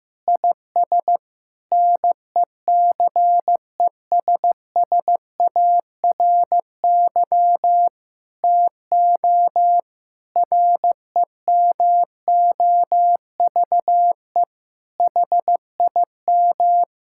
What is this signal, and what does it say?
Morse code, heard as a keyed tone, about 15 wpm, 700 Hz. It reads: IS NECESSARY TO REMOVE HIM
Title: Sentences from Top 1000 Words - Review Part 5 - 15wpm